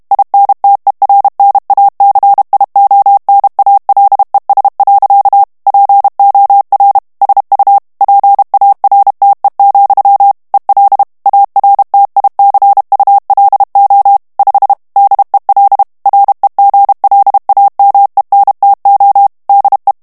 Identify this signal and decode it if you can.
Morse code, heard as a keyed tone, about 32 wpm, 800 Hz. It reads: INTERNACIONALES. POR SU PARTE, EL ARTICULO 5 DEL REGLAMENTO DE
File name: ESPAGNOL TEXTE EN CLAIR 2 VITESSE 30 MOTS MINUTES